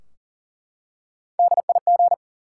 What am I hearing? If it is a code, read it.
Morse code, heard as a keyed tone, about 38 wpm, 700 Hz. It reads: DIG
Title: Three Letter Words - 40wpm